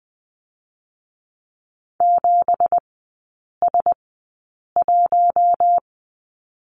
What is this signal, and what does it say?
Morse code, heard as a keyed tone, about 20 wpm, spaced as for 8 wpm, 690 Hz. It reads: 7S1
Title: 166 Lesson 27e - Number 7 - Three Characters 20wpm